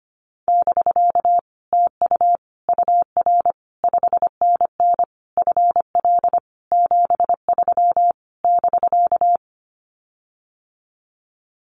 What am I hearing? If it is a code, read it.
Morse code, heard as a keyed tone, about 25 wpm, 700 Hz. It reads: <BK> TU UR 5NN FL 73 <BK>
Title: Common QSO Phrases Encore 25wpm